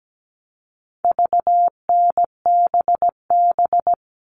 Morse code, heard as a keyed tone, about 17 wpm, 700 Hz. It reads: VNBB
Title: All Cut Numbers - 4-Digit 17wpm